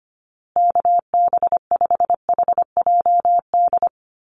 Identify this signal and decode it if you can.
Morse code, heard as a keyed tone, about 25 wpm, 700 Hz. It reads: KB5HJD
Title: USA Call Signs Repeated 2x 25wpm